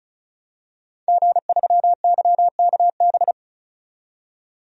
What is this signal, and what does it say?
Morse code, heard as a keyed tone, about 36 wpm, 700 Hz. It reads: G3YKB
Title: DXCC-100 Call Signs - 5 Characters - 35wpm